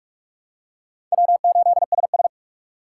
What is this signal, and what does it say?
Morse code, heard as a keyed tone, about 44 wpm, 700 Hz. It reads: W8SS
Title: USA Call Signs - 4 Characters - 3x Character Spacing - 45wpm